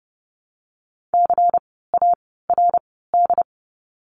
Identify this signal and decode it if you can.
Morse code, heard as a keyed tone, about 30 wpm, 700 Hz. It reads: C A R D
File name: Lesson 16f - Letter c - Words 30wpm